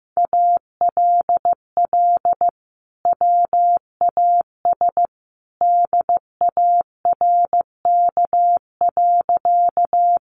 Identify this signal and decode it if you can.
Morse code, heard as a keyed tone, about 15 wpm, 700 Hz. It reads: ALL WAS DARK.
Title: Sentences from Top 500 Words 15wpm